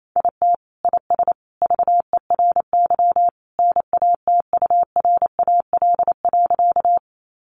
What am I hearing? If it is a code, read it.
Morse code, heard as a keyed tone, about 28 wpm, 700 Hz. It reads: IT IS VERY NATURAL.
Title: Sentences from Top 700 Words 28wpm